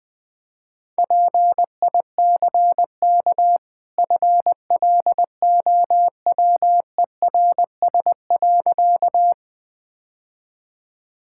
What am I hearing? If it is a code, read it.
Morse code, heard as a keyed tone, about 21 wpm, 700 Hz. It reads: PICK FLOWERS.
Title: Easy Sentences with Hints - 20wpm